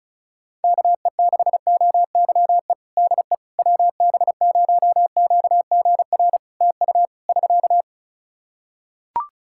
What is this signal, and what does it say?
Morse code, heard as a keyed tone, about 34 wpm, 700 Hz. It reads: KE6OYE DE WB0QGR TU <SK>
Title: Practice QSO - Exchange 4 - Ending - 35wpm